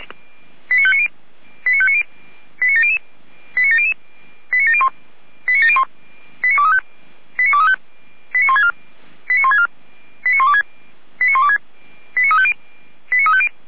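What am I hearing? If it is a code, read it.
Five-tone selective call (ZVEI), heard as quick runs of five tones, tones of about 70 ms. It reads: 89590 89590 89700 89700 89701 89701 89235 89235 89164 89164 89126 89126 89350 89350
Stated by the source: ZVEI selective call received on an SDR.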